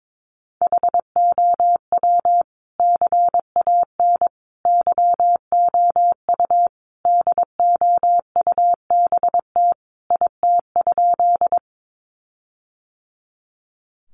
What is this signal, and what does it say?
Morse code, heard as a keyed tone, about 22 wpm, 700 Hz. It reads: HOW CAN YOU DOUBT IT?
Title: Sentences from Top 1500 Words 22wpm